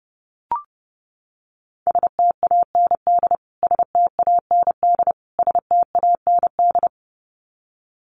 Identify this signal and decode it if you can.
Morse code, heard as a keyed tone, about 30 wpm, 700 Hz. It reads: STAND STAND STAND
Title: Top 500 Words Repeated 30wpm